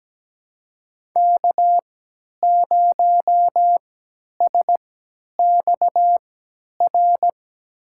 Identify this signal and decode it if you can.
Morse code, heard as a keyed tone, about 17 wpm, spaced as for 10 wpm, 700 Hz. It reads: K0SXR
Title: USA Call Signs - 5 Characters - 3x Character Spacing - 17wpm